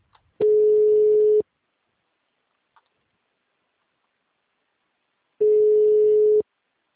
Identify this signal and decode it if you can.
Ringback tone (Europe, 425 Hz).